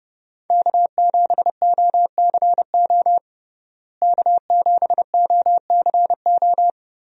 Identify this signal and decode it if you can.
Morse code, heard as a keyed tone, about 30 wpm, 700 Hz. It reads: K7OCO K7OCO
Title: USA Call Signs - 5 Characters - Repeated 3x - 30wpm